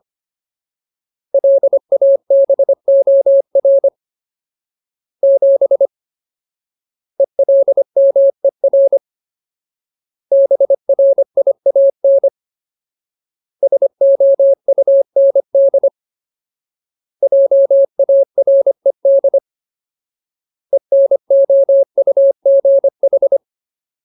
Morse code, heard as a keyed tone, about 25 wpm, 550 Hz. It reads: LABOR 7 ELMER BRIAN SOUND JARED ENOUGH
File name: Ultimate Word List / 25WPM Ultimate